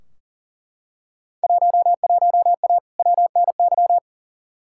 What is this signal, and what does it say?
Morse code, heard as a keyed tone, about 40 wpm, 700 Hz. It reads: 11A WNY
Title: Field Day Exchanges Encore - 40wpm